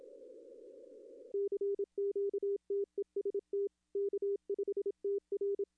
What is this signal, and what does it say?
Morse code, heard as a keyed tone, about 27 wpm, 400 Hz. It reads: CQTEST K5TR
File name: contests / arrl-160-k5tr-op-w5tn -1300